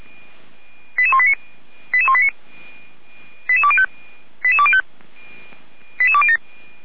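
Five-tone selective call (ZVEI), heard as quick runs of five tones, tones of about 70 ms. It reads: 88189 88189 88205 88205 88207
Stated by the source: ZVEI five-tone selective call 88169 received by SDR, logged by Feuerwehr Orlamünde.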